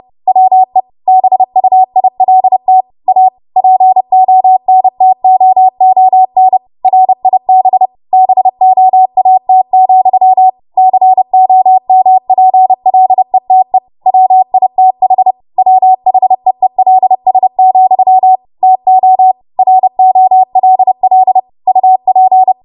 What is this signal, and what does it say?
Morse code, heard as a keyed tone, about 30 wpm, 750 Hz. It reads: WE BUILT A PONTOON RIB BOAT, COMPLETE WITH WHEELS, TO ROLL UP
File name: Morse / 30WPM